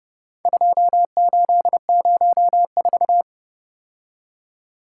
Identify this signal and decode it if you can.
Morse code, heard as a keyed tone, about 30 wpm, 700 Hz. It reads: 2804